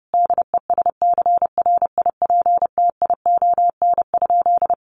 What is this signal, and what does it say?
Morse code, heard as a keyed tone, about 30 wpm, 700 Hz. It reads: DESCRIPTION?
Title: Sentences from Top 1500 Words 30wpm